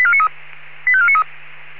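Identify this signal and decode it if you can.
Five-tone selective call (ZVEI), heard as quick runs of five tones, tones of about 70 ms. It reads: 85493
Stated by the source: ZVEI five-tone call 85492 received on SDR.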